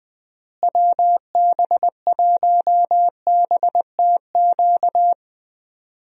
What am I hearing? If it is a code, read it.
Morse code, heard as a keyed tone, about 20 wpm, 700 Hz. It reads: WB1BTQ
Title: USA Call Signs - 6 Characters - Repeated 3x - 20wpm